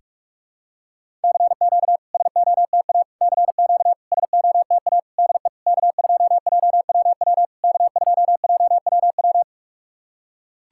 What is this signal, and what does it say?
Morse code, heard as a keyed tone, about 43 wpm, 700 Hz. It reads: CQ SOTA CQ SOTA DE K2JWW K2JWW
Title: SOTA QSO Practice 45wpm